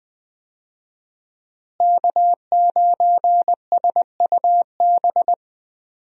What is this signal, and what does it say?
Morse code, heard as a keyed tone, about 20 wpm, 700 Hz. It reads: K9SUB
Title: USA Call Signs - 5 Characters - Repeated 2x - 20wpm